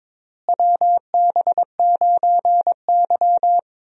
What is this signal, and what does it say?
Morse code, heard as a keyed tone, about 22 wpm, 700 Hz. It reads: WB9Y
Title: USA Call Signs - 4 Characters - Repeated 3x - 22wpm